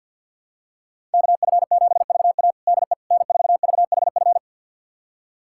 Morse code, heard as a keyed tone, about 46 wpm, 700 Hz. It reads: KF7VA DE N4VHF
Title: USA Callsign DE Callsign - 5 Characters - 50wpm